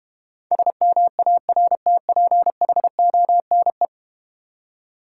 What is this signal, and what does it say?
Morse code, heard as a keyed tone, about 31 wpm, 700 Hz. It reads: SMARTPHONE
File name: Easy Words - Speed-Racing with Hints - 25wpm